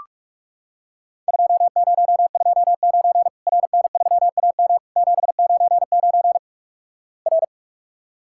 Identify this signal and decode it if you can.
Morse code, heard as a keyed tone, about 45 wpm, 700 Hz. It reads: 2029 RN3AM 799 R
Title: Worked All Europe DX Contest - QTC Exchange - 45wpm